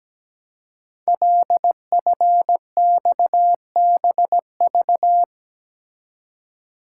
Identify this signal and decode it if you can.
Morse code, heard as a keyed tone, about 17 wpm, 700 Hz. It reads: LFXBV
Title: Code Groups - 5 Letters - 17wpm